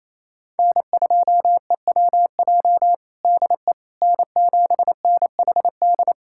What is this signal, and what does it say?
Morse code, heard as a keyed tone, about 27 wpm, 700 Hz. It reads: N2EWJ DE N7NHD
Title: USA Callsign DE Callsign - 5 Characters - 28wpm